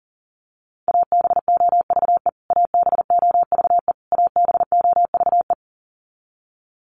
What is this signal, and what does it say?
Morse code, heard as a keyed tone, about 40 wpm, 700 Hz. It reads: ABOVE ABOVE ABOVE
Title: Top 500 Words Repeated 40wpm